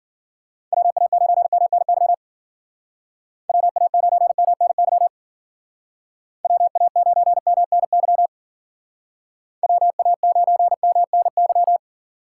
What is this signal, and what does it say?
Morse code, heard as a keyed tone, about 52 wpm, 700 Hz. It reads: WA9MNY WA9MNY WA9MNY WA9MNY